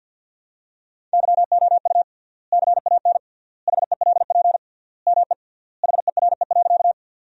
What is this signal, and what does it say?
Morse code, heard as a keyed tone, about 50 wpm, 700 Hz. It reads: YOU CAN HELP ME HERE.
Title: 3x Word-Spacing - Sentences from Top 200 Words - 50wpm